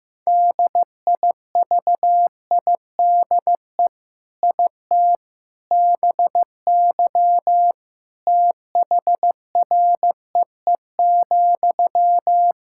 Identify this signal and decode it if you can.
Morse code, heard as a keyed tone, about 16 wpm, 700 Hz. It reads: DIVIDE IT BY THREE,
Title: Sentences from Top 1600 Words 15wpm